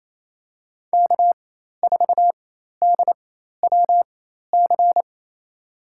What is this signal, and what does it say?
Morse code, heard as a keyed tone, about 28 wpm, spaced as for 13 wpm, 700 Hz. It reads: K4DWC
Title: USA Call Signs - 4x Character Spacing - 28wpm